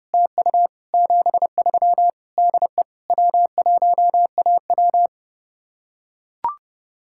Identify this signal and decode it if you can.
Morse code, heard as a keyed tone, about 32 wpm, 700 Hz. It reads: TU 73 DE W1AW